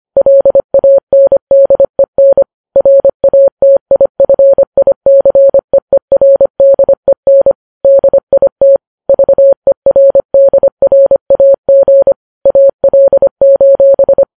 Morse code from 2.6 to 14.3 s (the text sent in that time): RATIFICEERDEN DIT VERDRAG AL: